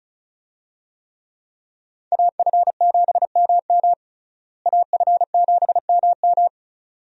Morse code, heard as a keyed tone, about 34 wpm, 700 Hz. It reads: AF7MM AF7MM
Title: USA Call Signs - 5 Characters - Repeated 2x - 35wpm